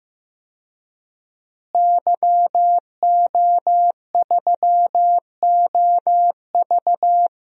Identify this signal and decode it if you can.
Morse code, heard as a keyed tone, about 15 wpm, 700 Hz. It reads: YO3OV
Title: DXCC-100 Call Signs - 5 Characters - Repeated 3x - 15wpm